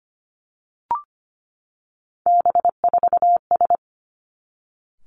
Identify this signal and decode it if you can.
Morse code, heard as a keyed tone, about 25 wpm, 700 Hz. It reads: B4S
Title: Three Challenging Letter-Number Combinations 25wpm